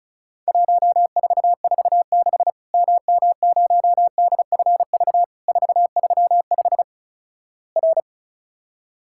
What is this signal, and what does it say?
Morse code, heard as a keyed tone, about 34 wpm, 700 Hz. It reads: 1446 MM0DFV 435 R
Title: Worked All Europe DX Contest - QTC Exchange - 35wpm